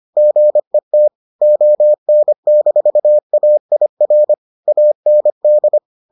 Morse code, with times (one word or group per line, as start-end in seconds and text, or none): 0.2-1.1 GET
1.4-4.3 ON-AIR
4.7-5.8 AND